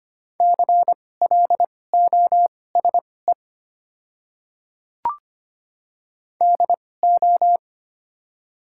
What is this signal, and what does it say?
Morse code, heard as a keyed tone, about 25 wpm, spaced as for 18 wpm, 700 Hz. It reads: CLOSE DO
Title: 2x Character Spacing - Top 300 Words - 25wpm